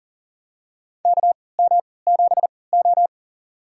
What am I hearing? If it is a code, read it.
Morse code, heard as a keyed tone, about 40 wpm, spaced as for 23 wpm, 700 Hz. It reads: KM7O
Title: USA Call Signs - 4 Characters - 3x Character Spacing - 40wpm